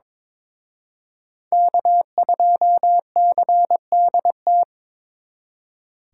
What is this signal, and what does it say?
Morse code, heard as a keyed tone, about 26 wpm, 700 Hz. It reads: K2CDT
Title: Speed Racing - USA 1x3 Call Signs 22wpm